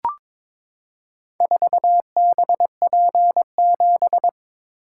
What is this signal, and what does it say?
Morse code, heard as a keyed tone, about 22 wpm, 700 Hz. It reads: 4BP7
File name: Four Letter-Number Combinations - 22wpm